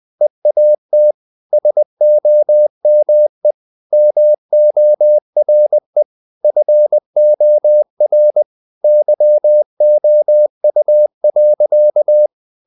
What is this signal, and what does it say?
Morse code, heard as a keyed tone, about 20 wpm, 600 Hz. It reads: EAT SOME MORE FOR YOU.
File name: the old man and the sea / the old man and the sea part 1 20wpm